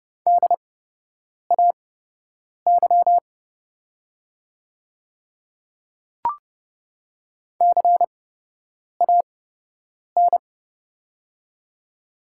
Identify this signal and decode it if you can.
Morse code, heard as a keyed tone, about 30 wpm, spaced as for 8 wpm, 700 Hz. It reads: DAY CAN
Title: 8x Character Spacing - Top 100 Words - 30wpm